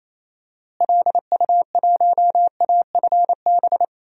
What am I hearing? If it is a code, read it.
Morse code, heard as a keyed tone, about 28 wpm, 700 Hz. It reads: LU1AFB
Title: DXCC-100 Call Signs - 6 Characters - Repeated 3x - 28wpm